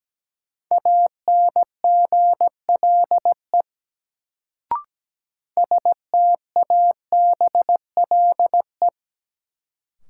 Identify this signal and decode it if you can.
Morse code, heard as a keyed tone, about 17 wpm, 700 Hz. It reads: ANGLE STABLE
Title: Sentences from Top 1500 Words - New Word Drill - 17wpm